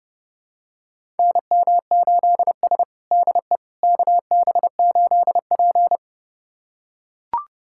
Morse code, heard as a keyed tone, about 32 wpm, 700 Hz. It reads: NM8S DE KB8P
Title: USA Callsign DE Callsign - 4 Characters - 30wpm